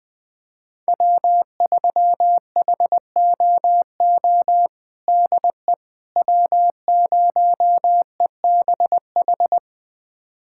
Morse code, heard as a keyed tone, about 20 wpm, 700 Hz. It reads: W3HOO DE W0EBH
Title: USA Callsign DE Callsign - 5 Characters - 20wpm